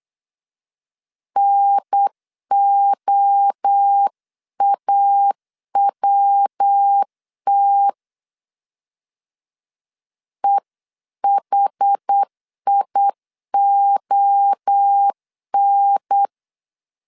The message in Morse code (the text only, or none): NOAWT EHION